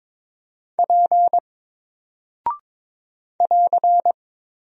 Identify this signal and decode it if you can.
Morse code, heard as a keyed tone, about 22 wpm, 700 Hz. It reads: P <AR>
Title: Single Character (Letter:Number:Punctuation:Prosign) 22wpm